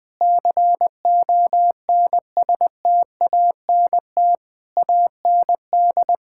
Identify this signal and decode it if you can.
Morse code, heard as a keyed tone, about 20 wpm, 700 Hz. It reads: CONSTANT AND
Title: Sentences from Top 1600 Words 20wpm